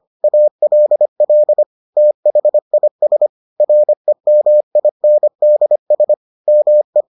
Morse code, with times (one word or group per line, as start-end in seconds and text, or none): 0.2-1.6 ALL
2.0-3.3 THIS
3.6-6.1 REMINDS
6.5-7.0 ME